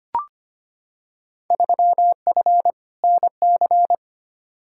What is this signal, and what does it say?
Morse code, heard as a keyed tone, about 25 wpm, 700 Hz. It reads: 3F NC
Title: Field Day Exchanges - 25wpm